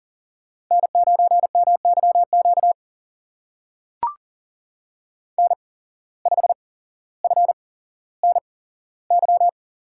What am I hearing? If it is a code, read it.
Morse code, heard as a keyed tone, about 39 wpm, spaced as for 17 wpm, 700 Hz. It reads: N9MYQ N5FNY
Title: USA Call Signs - 5 Characters - 8x Character Spacing - 40wpm